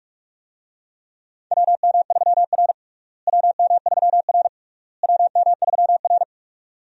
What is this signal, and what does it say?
Morse code, heard as a keyed tone, about 46 wpm, 700 Hz. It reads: WM3R WM3R WM3R
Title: USA Call Signs - 4 Characters - Repeated 3x - 45wpm